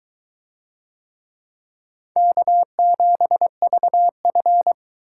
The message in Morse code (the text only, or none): K7VF